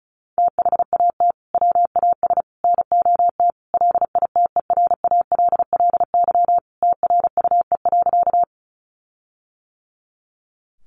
Morse code, heard as a keyed tone, about 34 wpm, 700 Hz. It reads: THAT WAS NOT LITERALLY TRUE.
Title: Sentences from Top 1400 Words 35wpm